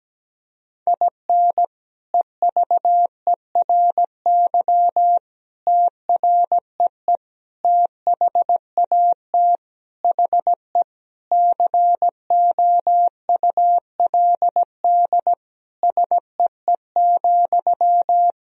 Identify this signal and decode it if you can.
Morse code, heard as a keyed tone, about 17 wpm, 700 Hz. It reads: IN EVERY TREE THAT HE COULD SEE,
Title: Sentences from Top 1000 Words - Review Part 1 - 17wpm